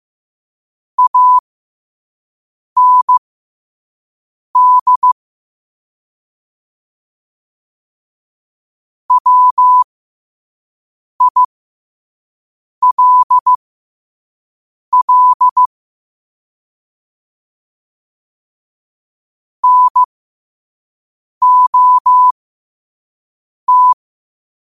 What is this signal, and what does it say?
Morse code, heard as a keyed tone, about 15 wpm, spaced as for 5 wpm, 1000 Hz. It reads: AND WILL NOT
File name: qotdfiles / 5wpm